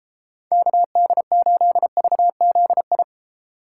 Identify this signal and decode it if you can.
Morse code, heard as a keyed tone, about 32 wpm, 700 Hz. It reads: KD8VZI